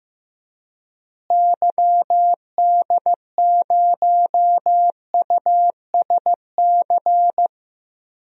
Morse code, heard as a keyed tone, about 15 wpm, 700 Hz. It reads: YD0USC
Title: DXCC-100 Call Signs - 6 Characters - Repeated 2x - 15wpm